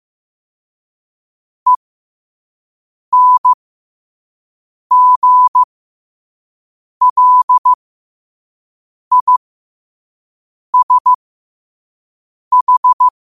Morse code, heard as a keyed tone, about 15 wpm, spaced as for 5 wpm, 1000 Hz. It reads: ENGLISH
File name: qotdfiles / 5wpm